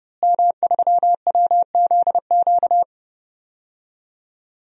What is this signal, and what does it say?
Morse code, heard as a keyed tone, about 29 wpm, 700 Hz. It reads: M3WZQ
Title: DXCC-100 Call Signs - 5 Characters - Repeated 2x - 30wpm